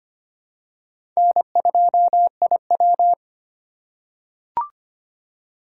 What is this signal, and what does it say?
Morse code, heard as a keyed tone, about 25 wpm, 700 Hz. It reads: N2IW
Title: USA Call Signs - 4 Characters - 2x Character Spacing - 25wpm